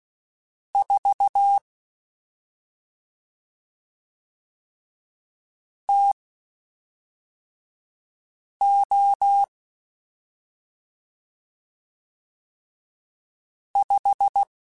Morse code, heard as a keyed tone, about 16 wpm, spaced as for 3 wpm, 800 Hz. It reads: 4 TO 5
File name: Spaced Characters / Spaced Characters-mp3s / Weather 4 21 sp @5